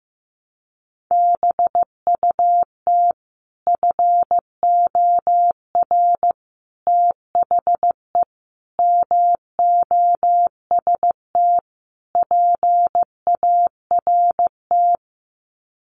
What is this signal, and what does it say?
Morse code, heard as a keyed tone, about 14 wpm, 700 Hz. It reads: BUT FOR THE MOST PART
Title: Sets of 5 Words Encore 15wpm